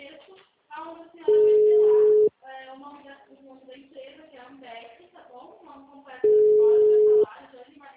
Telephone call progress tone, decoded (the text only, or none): ringback tone